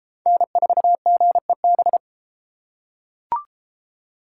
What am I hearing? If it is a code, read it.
Morse code, heard as a keyed tone, about 32 wpm, 700 Hz. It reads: N4GEB